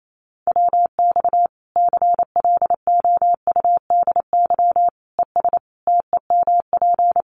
Morse code, heard as a keyed tone, about 28 wpm, 700 Hz. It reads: WX CLOUDY ES TEMP
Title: Common QSO Phrases 28wpm